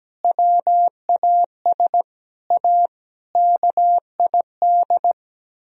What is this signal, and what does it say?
Morse code, heard as a keyed tone, about 17 wpm, 700 Hz. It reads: WAS A KID
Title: Speed-Racing - Sets of 5 Words 17wpm